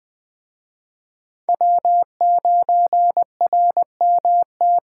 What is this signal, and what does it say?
Morse code, heard as a keyed tone, about 20 wpm, 700 Hz. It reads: W9RMT
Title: Speed Racing - USA 1x3 Call Signs 17wpm